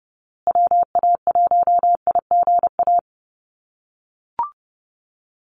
Morse code, heard as a keyed tone, about 30 wpm, 700 Hz. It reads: WA1IGA
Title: USA Call Signs Repeated 3x 30wpm